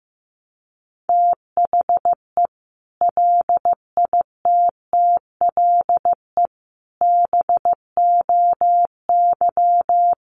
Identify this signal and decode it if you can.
Morse code, heard as a keyed tone, about 14 wpm, 700 Hz. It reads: THE LITTLE BOY
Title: Sentences from Top 500 Words 15wpm